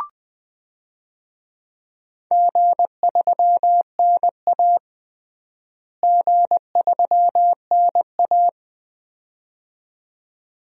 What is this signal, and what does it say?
Morse code, heard as a keyed tone, about 20 wpm, 700 Hz. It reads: G3NA G3NA
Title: DXCC-100 Call Signs - 4 Characters - Repeated 2x - 20wpm